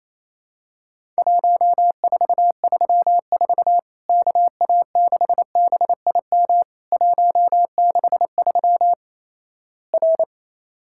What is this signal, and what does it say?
Morse code, heard as a keyed tone, about 28 wpm, 700 Hz. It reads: 1434 KA6BIM 163 R
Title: Worked All Europe DX Contest - QTC Exchange - 28wpm